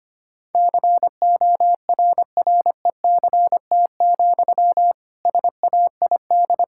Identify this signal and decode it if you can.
Morse code, heard as a keyed tone, about 25 wpm, 700 Hz. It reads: CORRECT, SAID
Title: Sentences from Top 1600 Words 25wpm